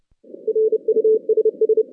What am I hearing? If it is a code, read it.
Morse code, heard as a keyed tone, about 30 wpm, 410 Hz. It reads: RUSS
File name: 2016 JAN CW NAQP / 2016 Jan naqp k5ot - 0-8